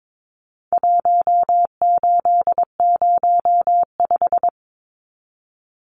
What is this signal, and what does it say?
Morse code, heard as a keyed tone, about 22 wpm, 700 Hz. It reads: 1805
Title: Numbers - Contest Style - 4-Digit 22wpm